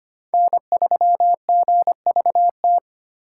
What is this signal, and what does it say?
Morse code, heard as a keyed tone, about 25 wpm, 700 Hz. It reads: N3GVT
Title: USA Call Signs - 5 Characters - 3x Character Spacing - 25wpm